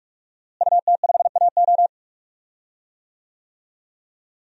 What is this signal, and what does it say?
Morse code, heard as a keyed tone, about 46 wpm, 700 Hz. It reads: UT5AO
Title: DXCC-100 Call Signs - 5 Characters - Repeated 3x - 45wpm